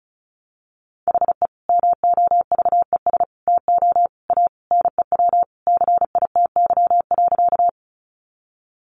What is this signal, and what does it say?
Morse code, heard as a keyed tone, about 34 wpm, 700 Hz. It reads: HE MOVES TO A NEW CITY.
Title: Easy Phrases 35wpm